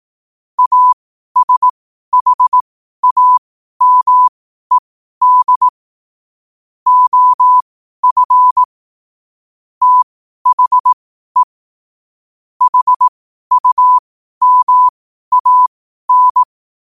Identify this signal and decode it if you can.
Morse code, heard as a keyed tone, about 18 wpm, spaced as for 12 wpm, 1000 Hz. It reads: ASHAMED OF THE HUMAN
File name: qotdfiles / 12wpm